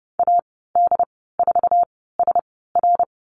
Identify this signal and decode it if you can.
Morse code, heard as a keyed tone, about 30 wpm, spaced as for 17 wpm, 700 Hz. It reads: AD4SR
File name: Lesson 15g - Number 5 - Call Signs 30wpm